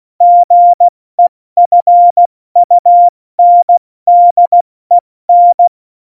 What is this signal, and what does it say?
Morse code, heard as a keyed tone, about 16 wpm, spaced as for 14 wpm, 700 Hz. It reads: GEFUNDEN
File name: Morsekurs / Witze / Witz 26. November 2019, 70 F80